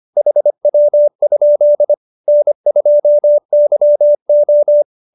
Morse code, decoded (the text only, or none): HW? N2YO